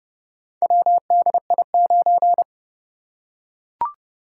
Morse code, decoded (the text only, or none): WDI9